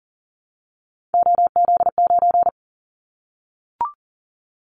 Morse code, heard as a keyed tone, about 40 wpm, 700 Hz. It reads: OZ9
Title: Callsign Prefixes for Top 100 DXCC - 40wpm